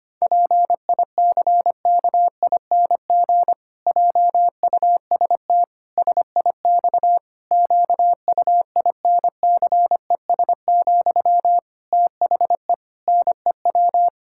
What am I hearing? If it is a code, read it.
Morse code, heard as a keyed tone, about 25 wpm, 700 Hz. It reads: PICKING JUST SIX QUINCES, THE NEW
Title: Pangram Challenge v2 - 25wpm